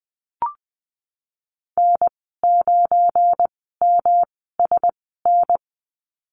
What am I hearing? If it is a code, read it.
Morse code, heard as a keyed tone, about 19 wpm, spaced as for 14 wpm, 700 Hz. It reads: N9MSN
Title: 179 Lesson 29g - Slash - Call Signs 20wpm